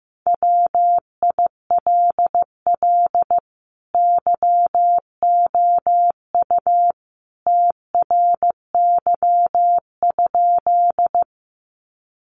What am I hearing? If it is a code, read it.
Morse code, heard as a keyed tone, about 15 wpm, 700 Hz. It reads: WILL YOU TRY?
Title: Sentences from Top 200 Words 15wpm